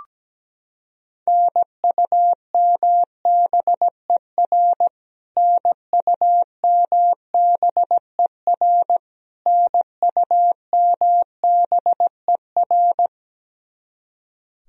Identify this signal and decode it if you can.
Morse code, heard as a keyed tone, about 17 wpm, 700 Hz. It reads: NUMBER NUMBER NUMBER
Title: Top 100 Words Repeated 17wpm